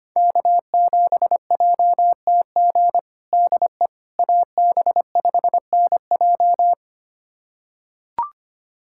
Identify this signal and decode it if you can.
Morse code, heard as a keyed tone, about 25 wpm, 700 Hz. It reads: K7JTG DE AB5NJ
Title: USA Callsign DE Callsign - 5 Characters - 25wpm